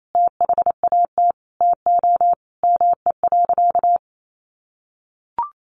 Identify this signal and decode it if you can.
Morse code, heard as a keyed tone, about 28 wpm, 700 Hz. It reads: THAT TO ME.
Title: Sentences from Top 400 Words 28wpm